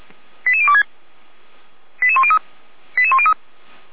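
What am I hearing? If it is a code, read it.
Five-tone selective call (ZVEI), heard as quick runs of five tones, tones of about 70 ms. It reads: 88026 88103 88103
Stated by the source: ZVEI five-tone selective call 88102 received by SDR, from the feuerwehr-orlamuende fire brigade audio server.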